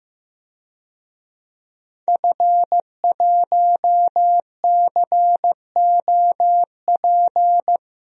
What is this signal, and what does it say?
Morse code, heard as a keyed tone, about 15 wpm, 700 Hz. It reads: F1COP
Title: DXCC-100 Call Signs - 5 Characters - Repeated 3x - 15wpm